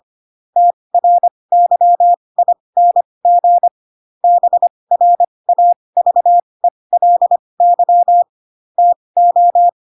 Morse code, heard as a keyed tone, about 25 wpm, spaced as for 20 wpm, 700 Hz. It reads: TRYING BRAVELY TO
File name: Pinocchio / Pinocchio ch1 15 25 ik1hge